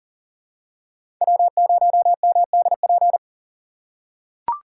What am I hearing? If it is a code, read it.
Morse code, heard as a keyed tone, about 40 wpm, 700 Hz. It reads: W0MDP
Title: USA Call Signs - 5 Characters - 3x Character Spacing - 40wpm